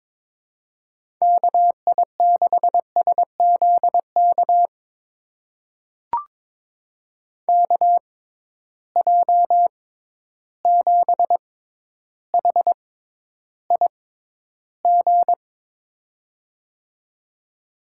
Morse code, heard as a keyed tone, about 22 wpm, spaced as for 11 wpm, 700 Hz. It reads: KI6SZK KJ7HIG
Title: USA Call Signs - 6 Characters - 6x Character Spacing - 22wpm